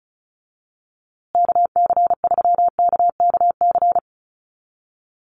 Morse code, heard as a keyed tone, about 34 wpm, 700 Hz. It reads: KC3KKC
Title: USA Call Signs Repeated 3x 35wpm